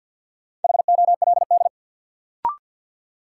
Morse code, heard as a keyed tone, about 53 wpm, 700 Hz. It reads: HOLD
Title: Sentences from Top 400 Words - New Word Drill - 50wpm